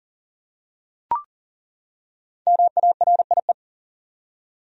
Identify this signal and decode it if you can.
Morse code, heard as a keyed tone, about 38 wpm, 700 Hz. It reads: MARIE
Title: USA Common Names - Max 5 Letters - 40wpm